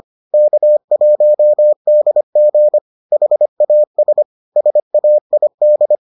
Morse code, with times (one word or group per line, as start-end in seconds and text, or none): 0.3-2.8 K1DG
3.1-4.2 HAS
4.6-6.0 SAID